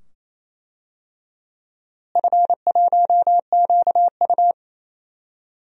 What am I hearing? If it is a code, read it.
Morse code, heard as a keyed tone, about 27 wpm, 700 Hz. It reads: F1QU
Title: DXCC-100 Call Signs - 4 Characters - Repeated 2x - 28wpm